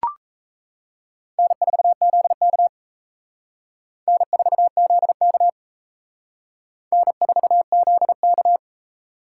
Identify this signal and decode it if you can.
Morse code, heard as a keyed tone, about 34 wpm, 700 Hz. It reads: N4ZK N4ZK N4ZK
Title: Speed Racing - USA 1x2 Call Signs 28wpm